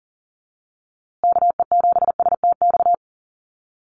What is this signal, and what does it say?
Morse code, heard as a keyed tone, about 40 wpm, 700 Hz. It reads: KE7STX
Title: USA Call Signs Repeated 2x 40wpm